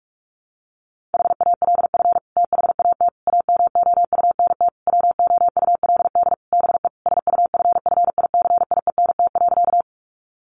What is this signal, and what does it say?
Morse code, heard as a keyed tone, about 44 wpm, 700 Hz. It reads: HALF THAT AMOUNT WOULD BE SUFFICIENT.
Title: Sentences from Top 1400 Words 45wpm